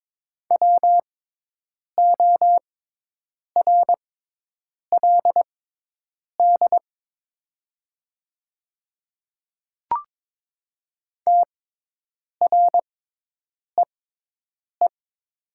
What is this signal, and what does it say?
Morse code, heard as a keyed tone, about 22 wpm, spaced as for 8 wpm, 700 Hz. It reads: WORLD TREE
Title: 6x Character Spacing - Top 300 Words - 22wpm